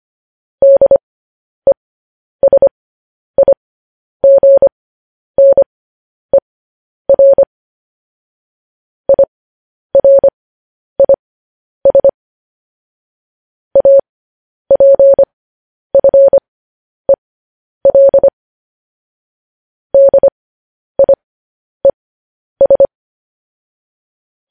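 Morse code, heard as a keyed tone, about 25 wpm, spaced as for 10 wpm, 550 Hz. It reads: DESIGNER IRIS APFEL DIES
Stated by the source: Morse code news headlines broadcast at 10 wpm.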